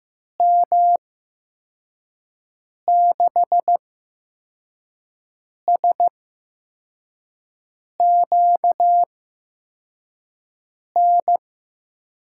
Morse code, heard as a keyed tone, about 15 wpm, spaced as for 4 wpm, 700 Hz. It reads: M6SQN
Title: USA Call Signs - 6 Characters - 8x Character Spacing - 15wpm